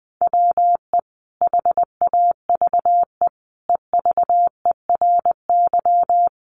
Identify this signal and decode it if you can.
Morse code, heard as a keyed tone, about 20 wpm, 700 Hz. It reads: WE HAVE EVERY
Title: Sentences from Top 1100 Words 20wpm